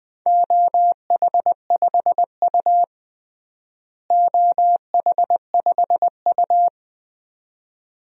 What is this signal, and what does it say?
Morse code, heard as a keyed tone, about 20 wpm, 700 Hz. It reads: OH5U OH5U
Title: DXCC-100 Call Signs - 4 Characters - Repeated 2x - 20wpm